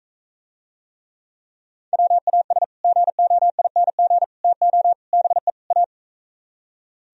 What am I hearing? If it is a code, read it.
Morse code, heard as a keyed tone, about 45 wpm, 700 Hz. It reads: WAS GOING TO BE A